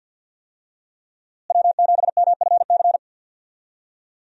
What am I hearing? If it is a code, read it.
Morse code, heard as a keyed tone, about 47 wpm, 700 Hz. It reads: W7MFC
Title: USA Call Signs - 5 Characters - Repeated 3x - 50wpm